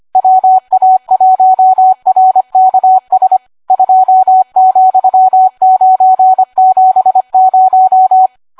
Morse code, heard as a keyed tone, about 25 wpm, 750 Hz, with radio noise. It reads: WA1RKS 2,970